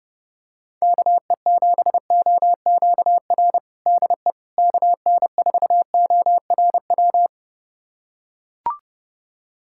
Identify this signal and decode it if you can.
Morse code, heard as a keyed tone, about 30 wpm, 700 Hz. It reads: KE7OQR DE KN4ORW
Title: USA Callsign DE Callsign - 6 Characters - 30wpm